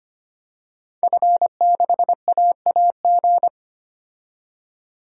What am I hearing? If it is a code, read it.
Morse code, heard as a keyed tone, about 25 wpm, 700 Hz. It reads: F6AAG
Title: DXCC-100 Call Signs - 5 Characters - 25wpm